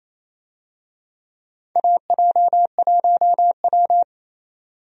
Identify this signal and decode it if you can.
Morse code, heard as a keyed tone, about 28 wpm, 700 Hz. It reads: AJ1W
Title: USA Call Signs - 4 Characters - 28wpm